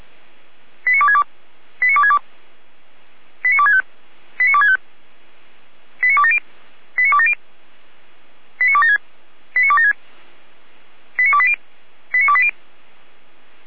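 Five-tone selective call (ZVEI), heard as quick runs of five tones, tones of about 70 ms. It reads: 89262 89262 89265 89265 89270 89270 89276 89276 89280 89280
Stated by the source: ZVEI five-tone selective call, code 89246, received via SDR.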